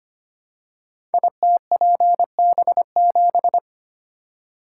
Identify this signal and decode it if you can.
Morse code, heard as a keyed tone, about 25 wpm, 700 Hz. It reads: ITPB7
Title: Code Groups - 5 Letter:Numbers - 25wpm